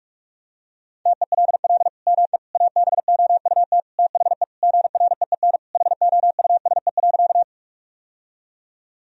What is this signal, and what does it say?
Morse code, heard as a keyed tone, about 44 wpm, 700 Hz. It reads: TELL ME ABOUT THE GREEN HOUSE.
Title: Sentences from Top 1000 Words - Review Part 2 - 45wpm